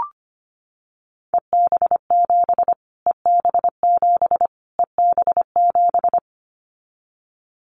Morse code, heard as a keyed tone, about 25 wpm, 700 Hz. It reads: EB7 EB7 EB7
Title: Callsign Prefixes for Top 25 DXCC - Repeated 3x - 25wpm